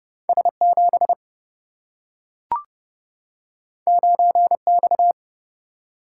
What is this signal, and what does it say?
Morse code, heard as a keyed tone, about 30 wpm, 700 Hz. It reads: S7 9X
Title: Single-Letter Single-Number 30wpm